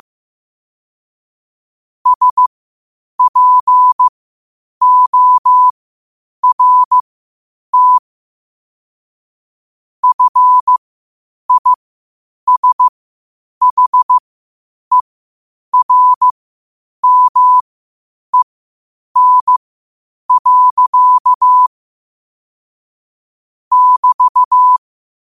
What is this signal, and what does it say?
Morse code, heard as a keyed tone, about 15 wpm, spaced as for 8 wpm, 1000 Hz. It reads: SPORT FISHERMEN. <BT>
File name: 8wpm